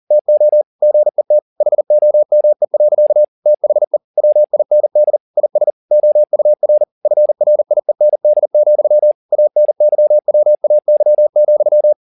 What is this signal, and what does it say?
Morse code, heard as a keyed tone, about 40 wpm, 600 Hz. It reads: TO GET HOME. THE WIND IS OUR FRIEND, ANYWAY,
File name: the old man and the sea 2 40wpm